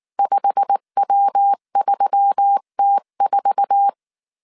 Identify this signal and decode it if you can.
Morse code, heard as a keyed tone, about 19 wpm, 800 Hz. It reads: 5W3T4
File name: CW / DET / 1 9 1500C S B 18wpm 2551